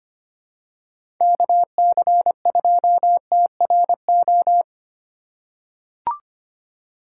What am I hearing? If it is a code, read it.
Morse code, heard as a keyed tone, about 25 wpm, 700 Hz. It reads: KC2TRO
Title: USA Call Signs - 6 Characters - 2x Character Spacing - 25wpm